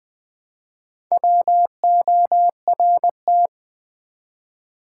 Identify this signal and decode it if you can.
Morse code, heard as a keyed tone, about 20 wpm, 690 Hz. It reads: WORT
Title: Build A Word - 1 Letter At A Time - 5-Letter Words - 20wpm